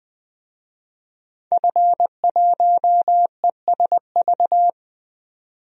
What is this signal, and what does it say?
Morse code, heard as a keyed tone, about 20 wpm, 700 Hz. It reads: F1ESV
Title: DXCC-100 Call Signs - 5 Characters - 20wpm